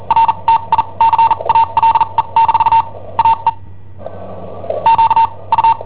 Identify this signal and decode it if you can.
Morse code, heard as a keyed tone, about 43 wpm, spaced as for 31 wpm, 910 Hz. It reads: RTICALE - AE QU
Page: Cassetta esercizi preparata da I7OHP per IW5ELP